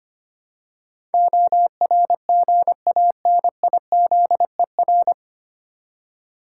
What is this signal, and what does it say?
Morse code, heard as a keyed tone, about 25 wpm, 700 Hz. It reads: ORGANIZER
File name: practice / 25wpm / 9 Letter Words - First 3 Letters Given - 25wpm